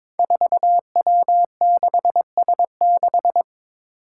Morse code, heard as a keyed tone, about 22 wpm, 700 Hz. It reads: 4W6S6
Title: Five Challenging Letter-Number Combinations 22wpm